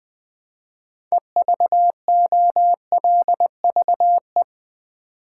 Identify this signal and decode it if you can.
Morse code, heard as a keyed tone, about 20 wpm, 700 Hz. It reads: EVOLVE
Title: Build A Word - 3 Letters At A Time - 6-Letter Words - 20wpm